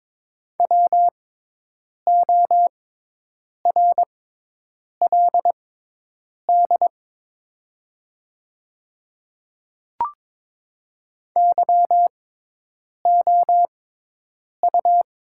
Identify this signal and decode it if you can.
Morse code, heard as a keyed tone, about 22 wpm, spaced as for 8 wpm, 700 Hz. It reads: WORLD YOU
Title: 6x Character Spacing - Top 200 Words - 22wpm